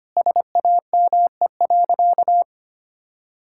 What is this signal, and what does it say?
Morse code, heard as a keyed tone, about 25 wpm, 700 Hz. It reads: SAME.
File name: practice / 25wpm / Sentences with 9-Word Phrases - 25wpm